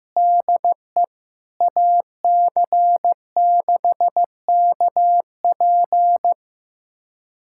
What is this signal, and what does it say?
Morse code, heard as a keyed tone, about 16 wpm, 700 Hz. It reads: DE AC6KP
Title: USA Callsign DE Callsign - 5 Characters - 15wpm